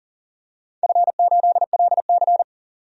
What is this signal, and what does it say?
Morse code, heard as a keyed tone, about 40 wpm, 700 Hz. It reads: F8LC
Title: DXCC-100 Call Signs - 4 Characters - 40wpm